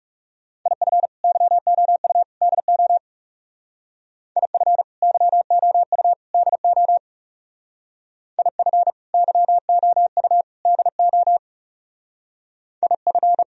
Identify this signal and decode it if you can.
Morse code, heard as a keyed tone, about 38 wpm, 700 Hz. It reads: IF YOU DO IF YOU DO IF YOU DO IF